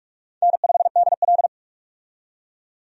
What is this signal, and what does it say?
Morse code, heard as a keyed tone, about 43 wpm, 700 Hz. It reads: N5DL
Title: USA Call Signs - 4 Characters - 45wpm